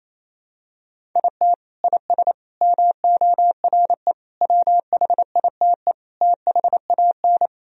Morse code, heard as a keyed tone, about 28 wpm, 700 Hz. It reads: IT IS MORE WHITE THAN
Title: Sentences from Top 1000 Words - Review Part 1 - 28wpm